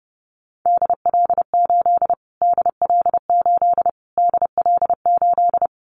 Morse code, heard as a keyed tone, about 30 wpm, 700 Hz. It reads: DL8 DL8 DL8
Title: Callsign Prefixes for Top 25 DXCC - Repeated 3x - 30wpm